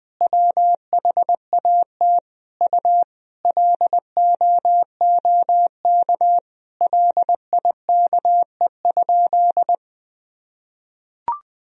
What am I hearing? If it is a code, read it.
Morse code, heard as a keyed tone, about 20 wpm, 700 Hz. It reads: WHAT U LOOK LIKE?
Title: Sentences from Top 100 Words and Abbreviations 20wpm